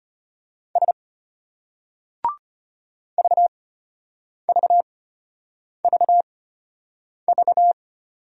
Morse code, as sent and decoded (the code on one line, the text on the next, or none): ... / ...- ...- ...- ...-
S VVVV